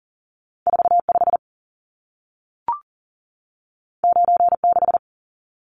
Morse code, heard as a keyed tone, about 40 wpm, 700 Hz. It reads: 45 96
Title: Numbers - 2-Digit 40wpm